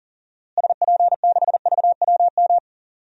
Morse code, heard as a keyed tone, about 37 wpm, 700 Hz. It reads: SP6VWM
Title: DXCC-100 Call Signs - 6 Characters - Repeated 2x - 40wpm